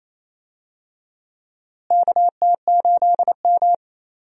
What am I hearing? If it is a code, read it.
Morse code, heard as a keyed tone, about 28 wpm, 700 Hz. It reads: KT8M
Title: USA Call Signs - 4 Characters - Repeated 3x - 28wpm